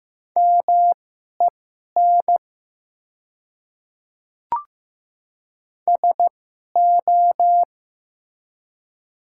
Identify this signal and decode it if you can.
Morse code, heard as a keyed tone, about 15 wpm, spaced as for 11 wpm, 700 Hz. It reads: MEN SO
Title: 2x Character Spacing - Top 200 Words - 15wpm